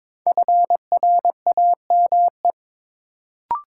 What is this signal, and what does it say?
Morse code, heard as a keyed tone, about 22 wpm, 700 Hz. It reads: FRAME
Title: Sentences from Top 1000 Words - Review Part 6 - New Word Drill - 22wpm